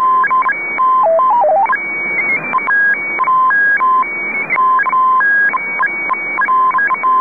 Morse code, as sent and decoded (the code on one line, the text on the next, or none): none